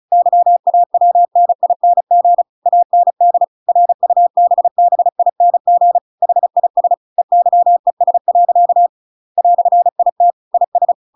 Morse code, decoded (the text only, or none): YAWNING AND RUBBING HIS EYES. "IT IS